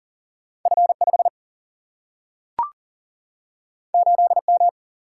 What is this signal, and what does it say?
Morse code, heard as a keyed tone, about 40 wpm, 700 Hz. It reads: F 5 8 M
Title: Single-Letter Single-Number 40wpm